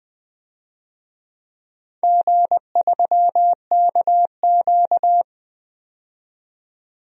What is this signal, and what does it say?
Morse code, heard as a keyed tone, about 20 wpm, 700 Hz. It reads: G3KQ
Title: DXCC-100 Call Signs - 4 Characters - 20wpm